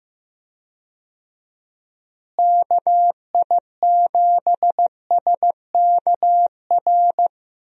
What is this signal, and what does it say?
Morse code, heard as a keyed tone, about 15 wpm, 700 Hz. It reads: KI7SKR
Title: USA Call Signs - 6 Characters - Repeated 2x - 15wpm